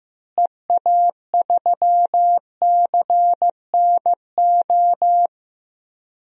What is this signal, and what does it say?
Morse code, heard as a keyed tone, about 15 wpm, 700 Hz. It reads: EA3CNO
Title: DXCC-100 Call Signs - 6 Characters - Repeated 2x - 15wpm